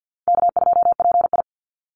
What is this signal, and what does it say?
Morse code, heard as a keyed tone, about 50 wpm, 700 Hz. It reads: K2PI
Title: Easy Call Signs - 50wpm